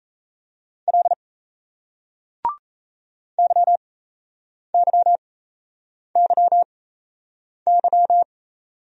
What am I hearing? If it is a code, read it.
Morse code, heard as a keyed tone, about 38 wpm, spaced as for 8 wpm, 700 Hz. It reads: L YYYY